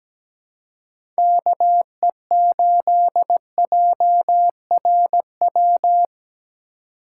Morse code, heard as a keyed tone, about 17 wpm, 700 Hz. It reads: KE8JRW
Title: USA Call Signs - 3x Character Spacing - 17wpm